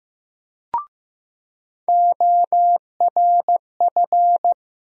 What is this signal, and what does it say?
Morse code, heard as a keyed tone, about 15 wpm, 700 Hz. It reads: ORF
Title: USA Call Sign Postfixes - 3 Letters 15wpm